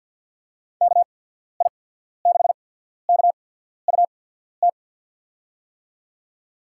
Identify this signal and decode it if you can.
Morse code, heard as a keyed tone, about 51 wpm, spaced as for 14 wpm, 700 Hz. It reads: KI6KUT
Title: USA Call Signs - 6 Characters - 8x Character Spacing - 50wpm